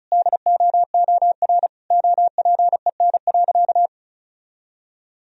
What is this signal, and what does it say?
Morse code, heard as a keyed tone, about 34 wpm, 700 Hz. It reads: DOOR OPEN.
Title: Sentences from Top 1000 Words - Review Part 1 - 35wpm